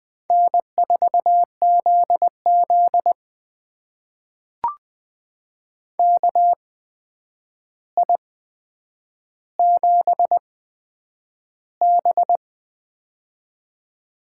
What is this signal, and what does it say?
Morse code, heard as a keyed tone, about 20 wpm, spaced as for 9 wpm, 700 Hz. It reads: N4ZZ KI7B
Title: USA Call Signs - 4 Characters - 8x Character Spacing - 20wpm